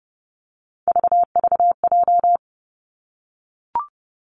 Morse code, heard as a keyed tone, about 30 wpm, 700 Hz. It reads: VVJ
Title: Three Challenging Letter-Number Combinations 30wpm